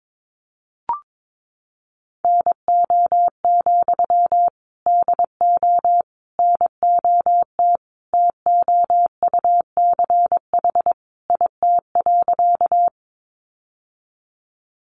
Morse code, heard as a keyed tone, about 22 wpm, 700 Hz. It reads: NO, DO NOT TOUCH IT.
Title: Sentences from Top 1200 Words 22wpm